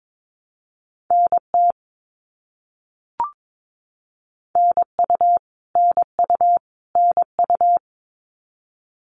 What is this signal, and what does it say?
Morse code, heard as a keyed tone, about 22 wpm, 700 Hz. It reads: NT NU NU NU
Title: Canadian Provinces and Territories - Repeated 3x - 22wpm